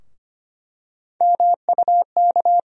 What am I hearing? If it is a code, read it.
Morse code, heard as a keyed tone, about 25 wpm, 700 Hz. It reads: MUK
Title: USA Call Sign Postfixes - 3 Letters 25wpm